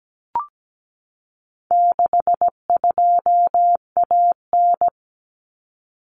Morse code, heard as a keyed tone, about 17 wpm, 700 Hz. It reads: 62AN
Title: Common Cut Numbers - 4-Digit 17wpm